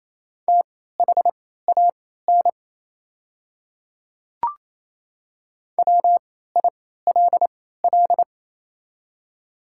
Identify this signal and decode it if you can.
Morse code, heard as a keyed tone, about 28 wpm, spaced as for 16 wpm, 690 Hz. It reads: THAN WILL
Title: 3x Character Spacing - Top 100 Words - 28wpm